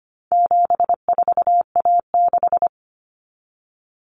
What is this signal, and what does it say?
Morse code, heard as a keyed tone, about 25 wpm, 700 Hz. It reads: 74A6
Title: Common Cut Numbers - 4-Digit 25wpm